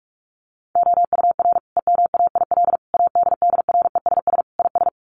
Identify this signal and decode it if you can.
Morse code, heard as a keyed tone, about 44 wpm, 700 Hz. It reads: OUR EMAIL ADDRESS IS
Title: Sets of 5 Words Encore2 45wpm